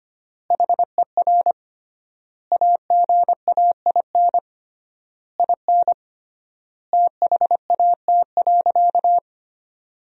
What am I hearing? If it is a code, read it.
Morse code, heard as a keyed tone, about 25 wpm, 700 Hz. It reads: HER AGAIN IN THAT.
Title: 3x Word-Spacing - Sentences from Top 200 Words - 25wpm